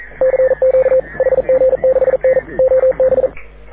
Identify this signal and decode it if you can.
Morse code, heard as a keyed tone, about 42 wpm, 530 Hz. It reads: CQ HG6N OB
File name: AUDIO2019 / HG6N CW SSB MIX